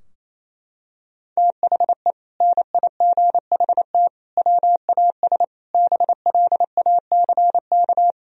Morse code, heard as a keyed tone, about 28 wpm, 700 Hz. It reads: THE NIGHT WAS BLACK
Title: Sentences from Top 1000 Words - Review Part 2 - 28wpm